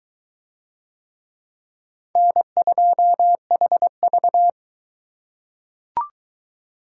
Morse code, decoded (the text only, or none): N2HV